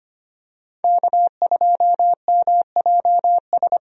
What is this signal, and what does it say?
Morse code, heard as a keyed tone, about 25 wpm, 700 Hz. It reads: K2MJS
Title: USA Call Signs - 5 Characters - Repeated 3x - 25wpm